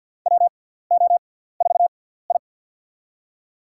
Morse code, heard as a keyed tone, about 47 wpm, spaced as for 17 wpm, 690 Hz. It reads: WO4I
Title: USA Call Signs - 6x Character Spacing - 50wpm